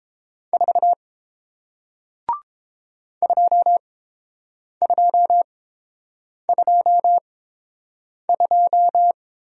Morse code, heard as a keyed tone, about 30 wpm, spaced as for 7 wpm, 700 Hz. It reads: 4 2222